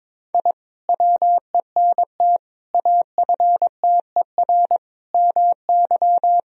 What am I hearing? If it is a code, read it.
Morse code, heard as a keyed tone, about 22 wpm, 700 Hz. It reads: I WENT AFTER MY
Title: Sentences from Top 1000 Words - Review Part 2 - 22wpm